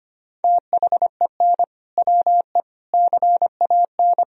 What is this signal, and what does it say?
Morse code, heard as a keyed tone, about 25 wpm, 700 Hz. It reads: THEN WE CAN
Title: Sentences from Top 1000 Words - Review Part 2 - 25wpm